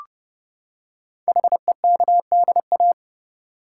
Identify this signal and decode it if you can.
Morse code, heard as a keyed tone, about 28 wpm, 700 Hz. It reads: HEKDA
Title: Code Groups - 5 Letters - 30wpm